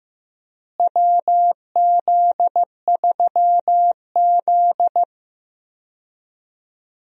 Morse code, heard as a keyed tone, about 15 wpm, 700 Hz. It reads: WZ3Z
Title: USA Call Signs - 4 Characters - Repeated 2x - 15wpm